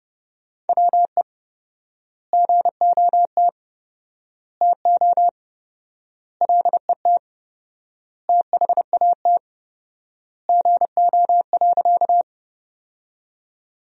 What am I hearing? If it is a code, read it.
Morse code, heard as a keyed tone, about 30 wpm, 700 Hz. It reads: WE GOT TO LET THAT GO.
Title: 4x Word-Spacing - Sentences from Top 300 Words - 30wpm